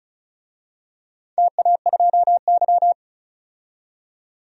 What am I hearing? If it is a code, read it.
Morse code, heard as a keyed tone, about 34 wpm, 700 Hz. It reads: TA2Y
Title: DXCC-100 Call Signs - 4 Characters - 35wpm